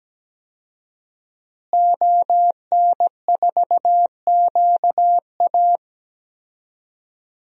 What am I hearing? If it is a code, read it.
Morse code, heard as a keyed tone, about 17 wpm, 700 Hz. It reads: ON4QA
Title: DXCC-100 Call Signs - 5 Characters - Repeated 2x - 17wpm